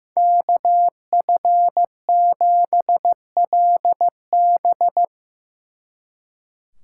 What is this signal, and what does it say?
Morse code, heard as a keyed tone, about 18 wpm, 700 Hz. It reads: KF7LB
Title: Speed Racing - USA 2x2 Call Signs 15wpm